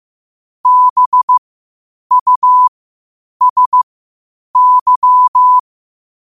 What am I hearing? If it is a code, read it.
Morse code, heard as a keyed tone, about 15 wpm, spaced as for 8 wpm, 1000 Hz. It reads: BUSY
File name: qotdfiles / 8wpm